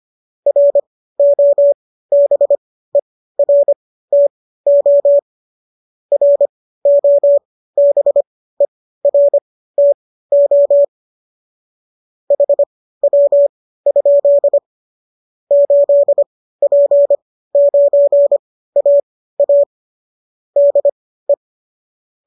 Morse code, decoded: ROBERTO ROBERTO HW? 8P9AA DE

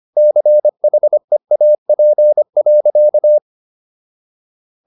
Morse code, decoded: CHEAP.

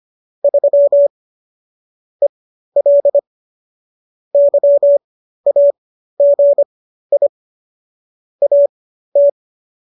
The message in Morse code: 3 EL YAGI AT

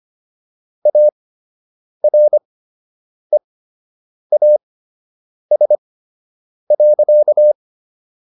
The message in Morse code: AREAS.

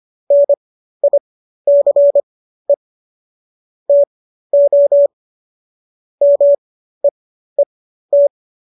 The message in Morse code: NICE TO MEET

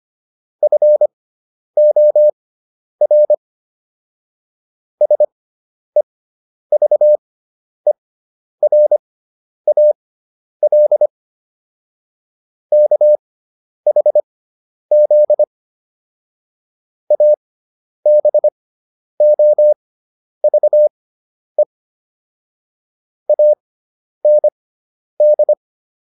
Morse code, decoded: FOR SEVERAL KHZ ABOVE AND

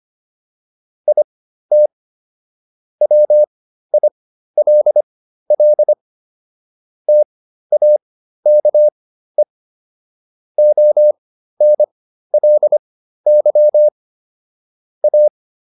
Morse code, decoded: IT WILL TAKE ONLY A

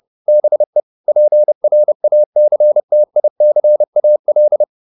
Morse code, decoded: BE PRACTICAL

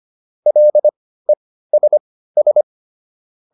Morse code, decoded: LESS

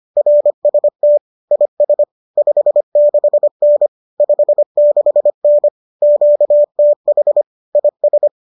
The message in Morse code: RST IS 56N 56N QTH IS

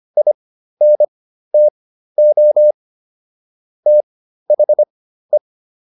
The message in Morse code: INTO THE

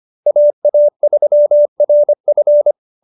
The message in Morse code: AA3RF

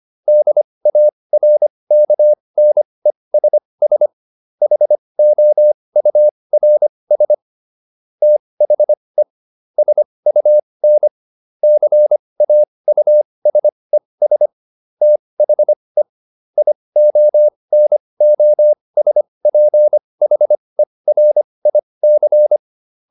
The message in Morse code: DARKNESS HOURS THE SUN CAUSES THE IONOSPHERIC

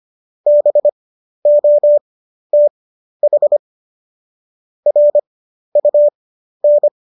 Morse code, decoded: BOTH RUN

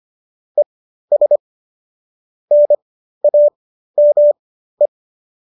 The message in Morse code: ES NAME

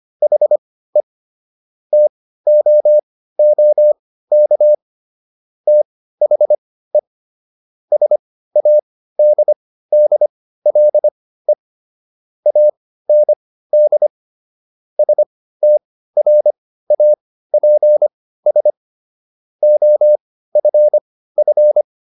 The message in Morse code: HE TOOK THE SADDLE AND STRAPS OFF

